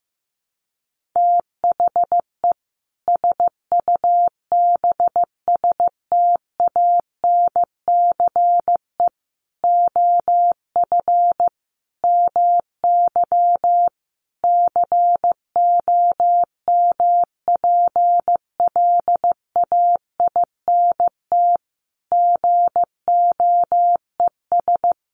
Morse code, heard as a keyed tone, about 15 wpm, 700 Hz. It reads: THE SUBSTANCE OF MY COMPLAINT GOES
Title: Sentences from Top 1500 Words 15wpm